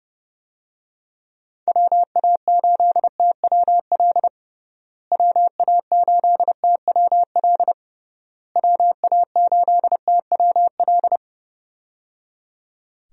Morse code, heard as a keyed tone, about 30 wpm, 700 Hz. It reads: WA8TWL WA8TWL WA8TWL
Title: USA Call Signs - 6 Characters - Repeated 3x - 30wpm